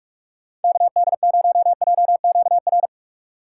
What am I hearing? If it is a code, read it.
Morse code, heard as a keyed tone, about 49 wpm, 700 Hz. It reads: KD0JQR